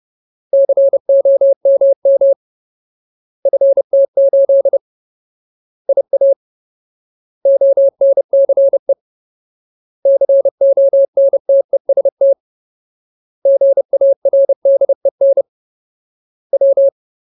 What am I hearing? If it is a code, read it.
Morse code, heard as a keyed tone, about 30 wpm, 550 Hz. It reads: COMM FT8 IA ONCE CONTEST GARDEN W
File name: Ultimate Word List / 30WPM Ultimate